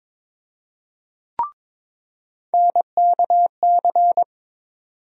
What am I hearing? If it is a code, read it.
Morse code, heard as a keyed tone, about 22 wpm, 700 Hz. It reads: NKC
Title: USA Call Sign Postfixes - 3 Letters 22wpm